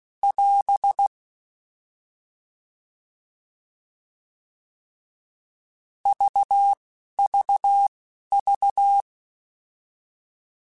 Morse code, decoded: <AS> VVV

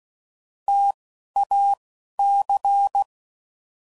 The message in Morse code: TAC

